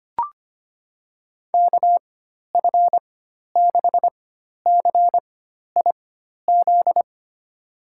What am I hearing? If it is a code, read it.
Morse code, heard as a keyed tone, about 25 wpm, spaced as for 12 wpm, 700 Hz. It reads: KF6CIZ
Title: USA Call Signs - 6 Characters - 4x Character Spacing - 25wpm